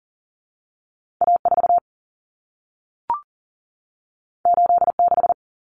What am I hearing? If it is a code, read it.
Morse code, heard as a keyed tone, about 40 wpm, 700 Hz. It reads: A4 86
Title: Common Cut Numbers - 2-Digit 40wpm